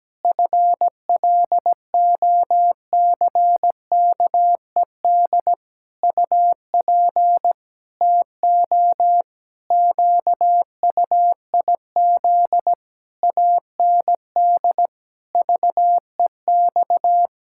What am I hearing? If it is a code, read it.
Morse code, heard as a keyed tone, about 17 wpm, 700 Hz. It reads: FLOCKED UP TO QUIZ AND VEX
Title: Pangram Challenge v2 - 17wpm